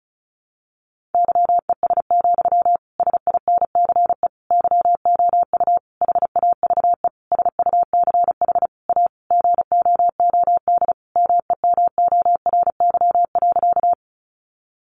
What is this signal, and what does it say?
Morse code, heard as a keyed tone, about 34 wpm, 700 Hz. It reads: YES, SINCE YOU HAVE SUCH A GOOD MEMORY.
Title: Sentences from Top 1300 Words 35wpm